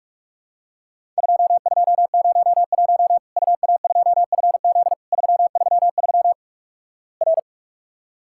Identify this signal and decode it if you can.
Morse code, heard as a keyed tone, about 46 wpm, 700 Hz. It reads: 2201 UA2FZ 333 R
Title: Worked All Europe DX Contest - QTC Exchange - 45wpm